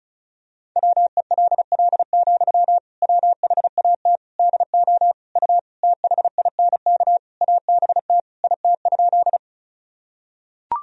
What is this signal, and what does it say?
Morse code, heard as a keyed tone, about 36 wpm, 700 Hz. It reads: WELL, WHAT DO U THINK ABT IT?
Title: Sentences from Top 200 Words and Abbreviations 35wpm